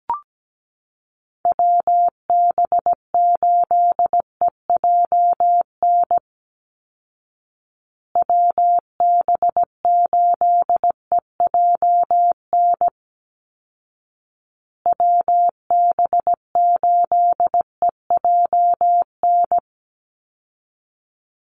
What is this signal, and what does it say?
Morse code, heard as a keyed tone, about 17 wpm, 700 Hz. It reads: WB8EJN WB8EJN WB8EJN
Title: Call Signs Repeated 2x - 17wpm